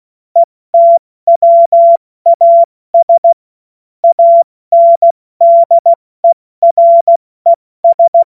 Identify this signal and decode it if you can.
Morse code, heard as a keyed tone, about 16 wpm, spaced as for 14 wpm, 680 Hz. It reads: ETWAS ANDERES
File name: Morsekurs / Witze / Witz 30. Mai 2023 70 F80